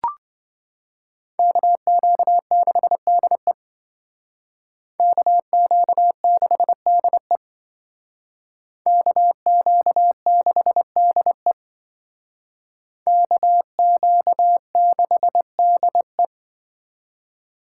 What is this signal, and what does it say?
Morse code, heard as a keyed tone, about 26 wpm, 700 Hz. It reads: KQ6DE KQ6DE KQ6DE KQ6DE